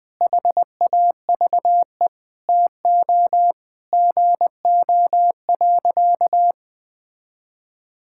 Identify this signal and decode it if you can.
Morse code, heard as a keyed tone, about 21 wpm, 700 Hz. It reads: HAVE TO GO.
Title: Sentences with 3-Word Phrases - Spoken Beginning - 20wpm